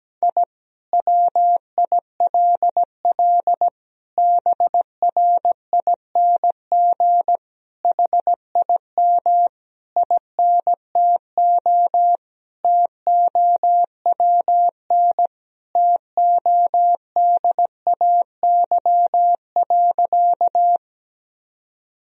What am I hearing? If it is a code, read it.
Morse code, heard as a keyed tone, about 17 wpm, 700 Hz. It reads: I WILL BRING HIM INTO TOWN TODAY.
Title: Sentences from Top 1000 Words - Review Part 3 - 17wpm